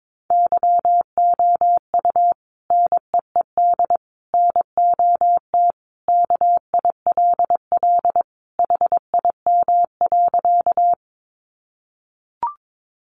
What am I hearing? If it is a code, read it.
Morse code, heard as a keyed tone, about 22 wpm, 700 Hz. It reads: YOU NEED NOT KILL HIM.
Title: Sentences from Top 1400 Words 22wpm